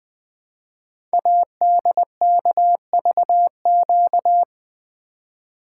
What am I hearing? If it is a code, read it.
Morse code, heard as a keyed tone, about 20 wpm, 700 Hz. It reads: ADKVQ
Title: Code Groups - 5 Letters - 20wpm